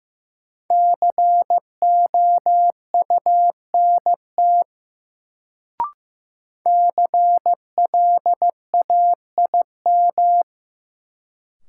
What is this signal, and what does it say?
Morse code, heard as a keyed tone, about 15 wpm, 700 Hz. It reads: COUNT CLAIM
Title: Sentences from Top 1500 Words - New Word Drill - 15wpm